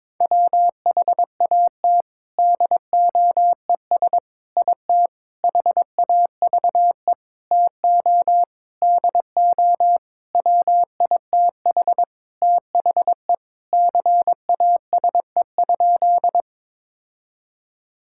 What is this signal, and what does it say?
Morse code, heard as a keyed tone, about 22 wpm, 700 Hz. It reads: WHAT DOES IT HAVE TO DO WITH THE CASE?
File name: practice / 22wpm / Sentences with 4-Word Phrases - 22wpm